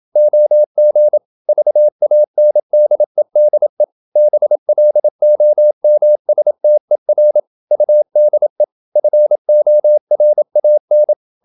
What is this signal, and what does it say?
Morse code, heard as a keyed tone, about 27 wpm, 600 Hz. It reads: OG VANDEDE BLOMSTER UDE FORAN
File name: Ferie QRQ 27wpm